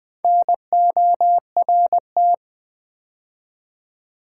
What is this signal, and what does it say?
Morse code, heard as a keyed tone, about 20 wpm, 700 Hz. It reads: NORT
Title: Build A Word - 2 Letters At A Time - 8-Letter Words - 20wpm